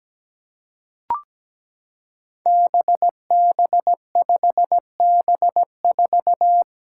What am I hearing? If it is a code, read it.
Morse code, heard as a keyed tone, about 17 wpm, 700 Hz. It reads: BB5B4
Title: Five Challenging Letter-Number Combinations 17wpm